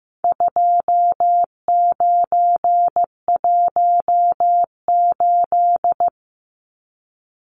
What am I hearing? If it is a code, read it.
Morse code, heard as a keyed tone, about 15 wpm, 700 Hz. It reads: 2918
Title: Numbers - 4-Digit 15wpm